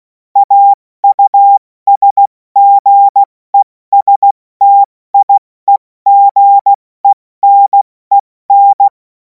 Morse code, heard as a keyed tone, about 16 wpm, 800 Hz. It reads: AUSGESTIEGENEN
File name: Morsekurs / Witze / Witz 31. März 2026 70 F80